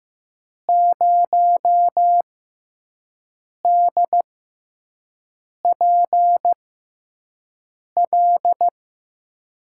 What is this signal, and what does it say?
Morse code, heard as a keyed tone, about 15 wpm, spaced as for 5 wpm, 700 Hz. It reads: 0DPL
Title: USA Call Signs - 5 Characters - 6x Character Spacing - 15wpm